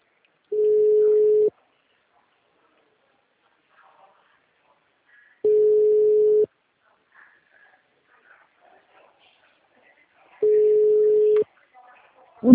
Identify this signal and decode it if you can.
Ringback tone (Europe, 425 Hz), with speech.